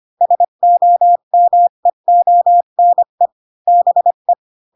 Morse code, heard as a keyed tone, about 25 wpm, 700 Hz. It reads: SOMEONE BE
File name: Pinocchio / Pinocchio ch1 20 25 ik1hge